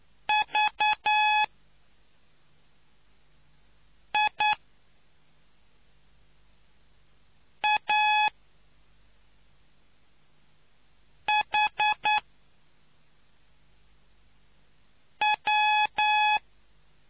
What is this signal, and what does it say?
Morse code, heard as a keyed tone, about 9 wpm, spaced as for 3 wpm, 850 Hz. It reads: VIAHW